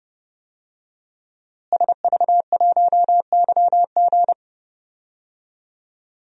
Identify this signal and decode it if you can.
Morse code, heard as a keyed tone, about 31 wpm, 700 Hz. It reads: SV1YG
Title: DXCC-100 Call Signs - 5 Characters - Repeated 2x - 30wpm